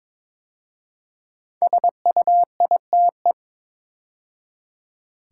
Morse code, heard as a keyed tone, about 22 wpm, 700 Hz. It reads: SUITE
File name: 5 Letter Words - First 3 Letters Given - 22wpm